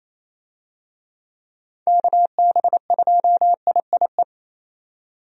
Morse code, heard as a keyed tone, about 27 wpm, 700 Hz. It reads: KB2IIE
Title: USA Call Signs - 6 Characters - 28wpm